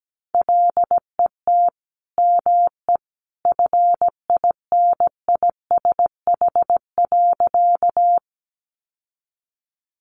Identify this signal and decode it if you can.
Morse code, heard as a keyed tone, about 17 wpm, 700 Hz. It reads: LET ME FINISH.
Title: Sentences from Top 1200 Words 17wpm